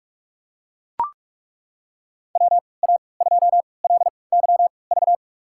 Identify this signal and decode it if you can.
Morse code, heard as a keyed tone, about 42 wpm, spaced as for 25 wpm, 700 Hz. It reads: WA2LYV
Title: USA Call Signs - 6 Characters - 3x Character Spacing - 45wpm